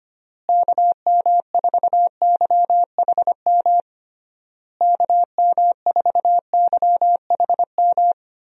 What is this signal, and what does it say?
Morse code, heard as a keyed tone, about 25 wpm, 700 Hz. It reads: KM4YHM KM4YHM
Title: USA Call Signs - 6 Characters - Repeated 3x - 25wpm